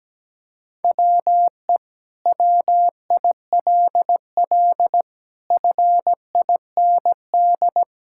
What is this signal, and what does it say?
Morse code, heard as a keyed tone, about 17 wpm, 700 Hz. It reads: WE WILL FIND
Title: Sentences from Top 1000 Words - Review Part 5 - 17wpm